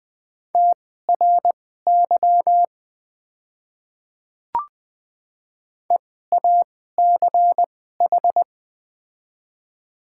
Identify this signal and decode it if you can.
Morse code, heard as a keyed tone, about 20 wpm, spaced as for 14 wpm, 700 Hz. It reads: TRY EACH
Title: 2x Character Spacing - Top 200 Words - 20wpm